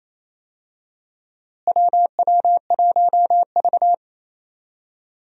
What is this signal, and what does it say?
Morse code, heard as a keyed tone, about 28 wpm, 700 Hz. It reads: WW1V
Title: USA Call Signs - 4 Characters - Repeated 2x - 28wpm